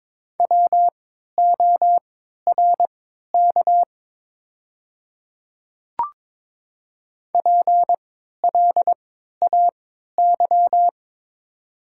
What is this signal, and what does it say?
Morse code, heard as a keyed tone, about 23 wpm, spaced as for 13 wpm, 700 Hz. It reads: WORK PLAY
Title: 3x Character Spacing - Top 300 Words - 22wpm